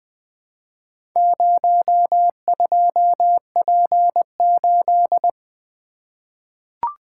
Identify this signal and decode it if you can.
Morse code, heard as a keyed tone, about 20 wpm, 700 Hz. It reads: 02P8
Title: Four Letter-Number Combinations - 20wpm